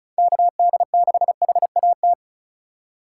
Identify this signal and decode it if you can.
Morse code, heard as a keyed tone, about 34 wpm, 700 Hz. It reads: KD6HAT
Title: USA Call Signs - 3x Character Spacing - 35wpm